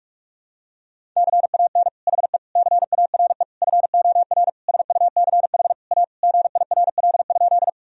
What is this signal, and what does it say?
Morse code, heard as a keyed tone, about 42 wpm, 700 Hz. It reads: CAN HE CARE FOR SUCH A GIRL?
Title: Sentences from Top 1000 Words - Review Part 1 - 45wpm